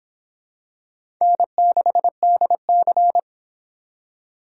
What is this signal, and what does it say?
Morse code, heard as a keyed tone, about 24 wpm, 700 Hz. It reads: N6DC